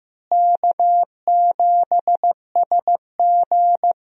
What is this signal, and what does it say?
Morse code, heard as a keyed tone, about 15 wpm, 700 Hz. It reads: K7SG
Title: USA Call Signs - 4 Characters - Repeated 3x - 15wpm